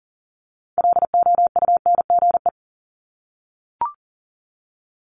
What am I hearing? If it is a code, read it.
Morse code, heard as a keyed tone, about 39 wpm, 700 Hz. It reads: LOUNGE
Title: Multi-letter Phonemes 40wpm